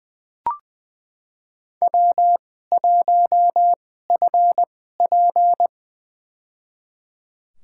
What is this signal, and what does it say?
Morse code, heard as a keyed tone, about 20 wpm, spaced as for 15 wpm, 700 Hz. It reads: W1FP
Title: USA Call Signs - 2x Character Spacing - 20wpm